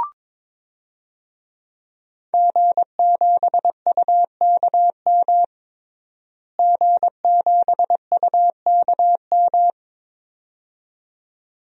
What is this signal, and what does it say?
Morse code, heard as a keyed tone, about 22 wpm, 700 Hz. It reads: G7UKM G7UKM
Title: DXCC-100 Call Signs - 5 Characters - Repeated 2x - 22wpm